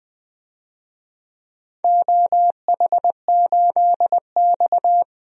Morse code, heard as a keyed tone, about 20 wpm, 700 Hz. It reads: OH8X
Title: DXCC-100 Call Signs - 4 Characters - 20wpm